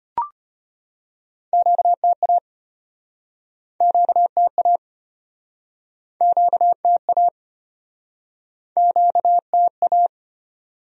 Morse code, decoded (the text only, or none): QTA QTA QTA QTA